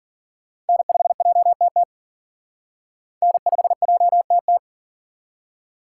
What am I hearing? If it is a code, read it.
Morse code, heard as a keyed tone, about 49 wpm, 700 Hz. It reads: N5JTT N5JTT
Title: Speed Racing - USA 1x3 Call Signs 40wpm